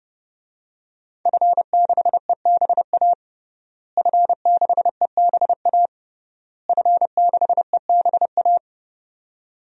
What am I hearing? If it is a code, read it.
Morse code, heard as a keyed tone, about 32 wpm, 700 Hz. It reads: F6EBA F6EBA F6EBA
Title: DXCC-100 Call Signs - 5 Characters - Repeated 3x - 30wpm